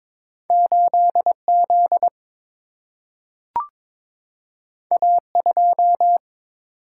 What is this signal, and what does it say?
Morse code, heard as a keyed tone, about 22 wpm, 700 Hz. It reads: 8 Z A 2
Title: Single-Letter Single-Number 22wpm